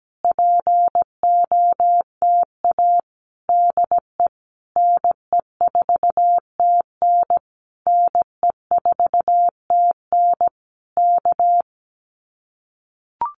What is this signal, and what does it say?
Morse code, heard as a keyed tone, about 17 wpm, 700 Hz. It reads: POTA DE NE4TN NE4TN K
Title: Common QSO Phrases Encore 17wpm